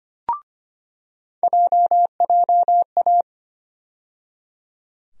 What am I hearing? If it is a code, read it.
Morse code, heard as a keyed tone, about 25 wpm, 700 Hz. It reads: JJA
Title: USA Call Sign Postfixes - 3 Letters 25wpm